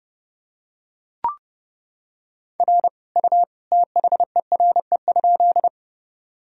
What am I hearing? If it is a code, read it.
Morse code, heard as a keyed tone, about 30 wpm, 700 Hz. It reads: R U THERE?
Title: Sentences from Top 100 Words and Abbreviations 30wpm